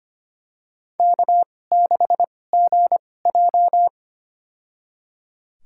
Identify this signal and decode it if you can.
Morse code, heard as a keyed tone, about 25 wpm, spaced as for 18 wpm, 700 Hz. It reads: K6GJ
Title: USA Call Signs - 2x Character Spacing - 25wpm